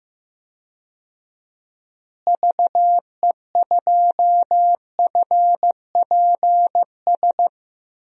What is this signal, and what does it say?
Morse code, heard as a keyed tone, about 15 wpm, 700 Hz. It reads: VE2FPS
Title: DXCC-100 Call Signs - 6 Characters - Repeated 3x - 15wpm